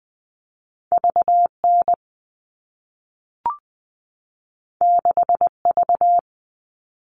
Morse code, heard as a keyed tone, about 20 wpm, 700 Hz. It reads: V N 6 V
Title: All Cut Numbers - 2-Digit 20wpm